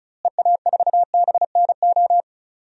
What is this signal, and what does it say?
Morse code, heard as a keyed tone, about 36 wpm, 700 Hz. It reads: EA4BNO
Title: DXCC-100 Call Signs - 6 Characters - Repeated 2x - 35wpm